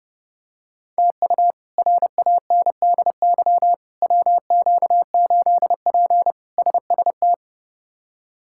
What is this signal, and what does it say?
Morse code, heard as a keyed tone, about 32 wpm, 700 Hz. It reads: TU RANDY WQ8P SST
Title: K1USN SST QSO Practice - 30wpm